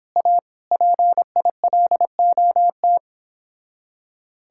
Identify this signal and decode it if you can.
Morse code, heard as a keyed tone, about 29 wpm, 700 Hz. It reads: A PILOT